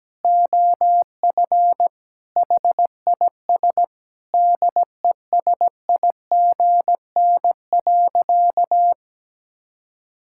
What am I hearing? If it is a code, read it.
Morse code, heard as a keyed tone, about 17 wpm, 700 Hz. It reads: OF HIS DESIGN.
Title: Sentences from Top 1000 Words - Review Part 6 - 17wpm